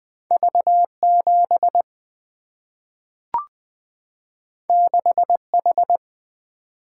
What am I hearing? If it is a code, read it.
Morse code, heard as a keyed tone, about 20 wpm, 700 Hz. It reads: V 7 6 H
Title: Single-Letter Single-Number 20wpm